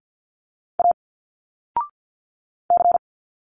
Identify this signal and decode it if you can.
Morse code, heard as a keyed tone, about 50 wpm, 700 Hz. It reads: A C
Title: Single Letters 50wpm